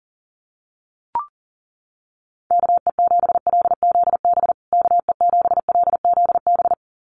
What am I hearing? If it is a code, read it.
Morse code, heard as a keyed tone, about 39 wpm, 700 Hz. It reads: KE7LZB KE7LZB
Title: USA Call Signs Repeated 2x 40wpm